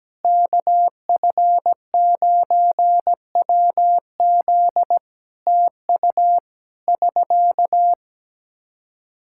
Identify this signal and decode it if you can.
Morse code, heard as a keyed tone, about 17 wpm, 700 Hz. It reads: KF9WZ TU <SK>
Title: Practice QSO - Exchange 4 - Ending - 17wpm